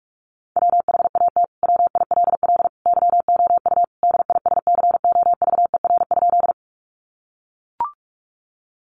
Morse code, heard as a keyed tone, about 44 wpm, 700 Hz. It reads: WHAT WILL YOU DISCOVER?
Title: Sentences from Top 1100 Words 45wpm